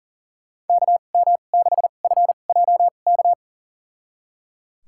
Morse code, heard as a keyed tone, about 39 wpm, spaced as for 29 wpm, 700 Hz. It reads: KM6FJK
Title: USA Call Signs - 2x Character Spacing - 40wpm